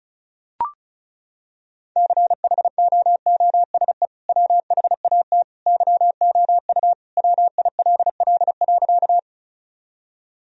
Morse code, heard as a keyed tone, about 34 wpm, 700 Hz. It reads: CHOOSE WHAT YOU WILL.
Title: Sentences from Top 1000 Words - Review Part 4 - 35wpm